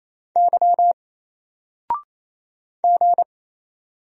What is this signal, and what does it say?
Morse code, heard as a keyed tone, about 28 wpm, 700 Hz. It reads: Y G
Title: Single Character (Letter:Number:Punctuation:Prosign) Alt 28wpm